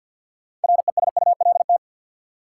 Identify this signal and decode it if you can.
Morse code, heard as a keyed tone, about 53 wpm, 700 Hz. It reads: RESULT
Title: Sentences from Top 700 Words - New Word Drill - 50wpm